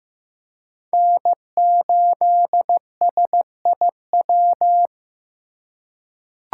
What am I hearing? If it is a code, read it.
Morse code, heard as a keyed tone, about 15 wpm, 700 Hz. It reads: N8SIW
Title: USA Call Signs - 5 Characters - 4x Character Spacing - 15wpm